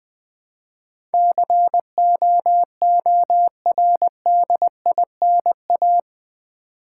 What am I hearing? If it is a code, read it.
Morse code, heard as a keyed tone, about 20 wpm, 700 Hz. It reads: COORDINA
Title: Build A Word - 4 Letters At A Time - 12-Letter Words - 20wpm